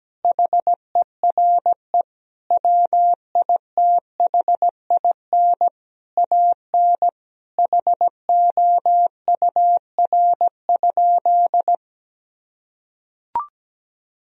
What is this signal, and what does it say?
Morse code, heard as a keyed tone, about 17 wpm, 700 Hz. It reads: HERE WITHIN AN HOUR?
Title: Sentences from Top 1000 Words - Review Part 4 - 17wpm